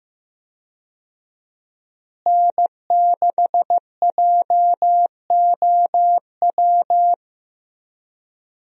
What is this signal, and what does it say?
Morse code, heard as a keyed tone, about 15 wpm, 700 Hz. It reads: N6JOW
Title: USA Call Signs - 5 Characters - Repeated 2x - 15wpm